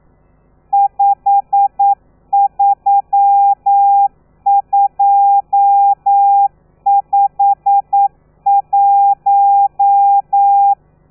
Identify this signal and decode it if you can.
Morse code, heard as a keyed tone, about 9 wpm, 800 Hz. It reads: 53251
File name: telegrafia / i2tfj / Lez12 40